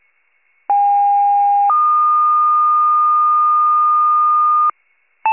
Two-tone page, heard as a one-second tone, then a three-second tone, about 810 then 1220 Hz. It is heard from 0.7 to 4.7 s.